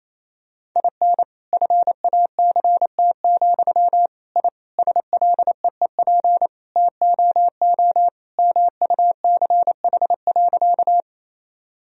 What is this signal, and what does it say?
Morse code, heard as a keyed tone, about 28 wpm, 700 Hz. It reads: IN FACT, I SLEEP TOO MUCH.
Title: Sentences from Top 1000 Words - Review Part 3 - 28wpm